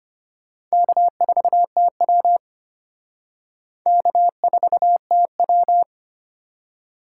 Morse code, heard as a keyed tone, about 32 wpm, 700 Hz. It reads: K4TW K4TW